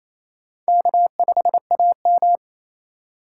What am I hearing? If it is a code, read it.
Morse code, heard as a keyed tone, about 27 wpm, 700 Hz. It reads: K5AM
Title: K1USN SST QSO Practice - 28wpm